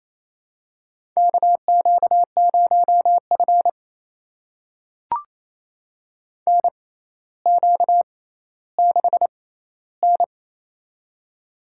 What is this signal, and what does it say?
Morse code, heard as a keyed tone, about 28 wpm, spaced as for 14 wpm, 700 Hz. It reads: KQ0F NQ6N
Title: USA Call Signs - 4 Characters - 6x Character Spacing - 28wpm